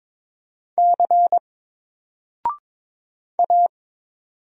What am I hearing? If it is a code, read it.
Morse code, heard as a keyed tone, about 22 wpm, 700 Hz. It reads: C A
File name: Single-Letter-Number-22wpm